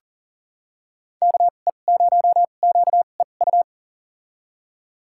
Morse code, heard as a keyed tone, about 40 wpm, spaced as for 29 wpm, 700 Hz. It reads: KE0QEU
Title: USA Call Signs - 6 Characters - 2x Character Spacing - 40wpm